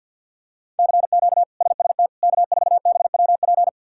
Morse code, heard as a keyed tone, about 46 wpm, 700 Hz. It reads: CQ SST K4BWP
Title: K1USN SST QSO Practice - 50wpm